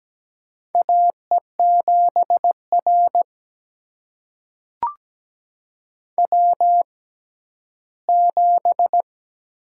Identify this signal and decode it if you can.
Morse code, heard as a keyed tone, about 17 wpm, spaced as for 11 wpm, 700 Hz. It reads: AE7R W7
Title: USA Call Signs - 6x Character Spacing - 17wpm